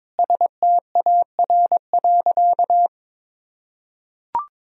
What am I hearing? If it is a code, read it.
Morse code, heard as a keyed tone, about 22 wpm, 700 Hz. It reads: STAR.
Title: Sentences from Top 1000 Words - Review Part 2 - 22wpm